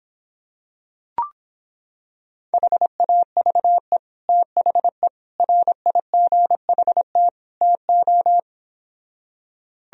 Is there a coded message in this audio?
Yes: Morse code from 2.5 to 8.4 s, heard as a keyed tone, about 26 wpm, 700 Hz. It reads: HAVE THE RIGHT TO